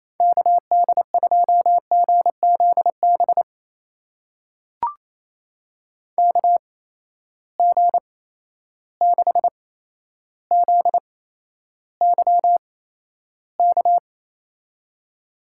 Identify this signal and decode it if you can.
Morse code, heard as a keyed tone, about 28 wpm, spaced as for 12 wpm, 700 Hz. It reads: KD2GZB KG6ZYK
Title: USA Call Signs - 6 Characters - 8x Character Spacing - 28wpm